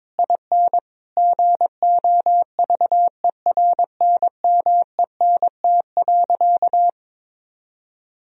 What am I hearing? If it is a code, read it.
Morse code, heard as a keyed tone, about 22 wpm, 700 Hz. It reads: IN GOVERNMENT.
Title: Sentences from Top 1000 Words - Review Part 4 - 22wpm